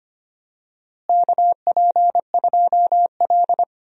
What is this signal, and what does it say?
Morse code, heard as a keyed tone, about 25 wpm, 700 Hz. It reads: KP2L
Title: DXCC-100 Call Signs - 4 Characters - Repeated 2x - 25wpm